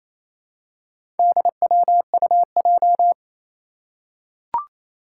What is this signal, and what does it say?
Morse code, heard as a keyed tone, about 27 wpm, 700 Hz. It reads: DWUJ
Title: Four Challenging Letter-Number Combinations 28wpm